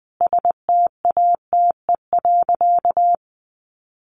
Morse code, heard as a keyed tone, about 20 wpm, 700 Hz. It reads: STATE.
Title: Sentences from Top 1300 Words 20wpm